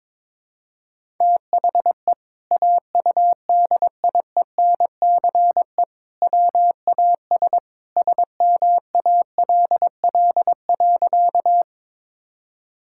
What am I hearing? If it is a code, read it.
Morse code, heard as a keyed tone, about 22 wpm, 700 Hz. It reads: THE AUDIENCE WAS SMALL.
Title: Sentences from Top 1000 Words - Review Part 5 - 22wpm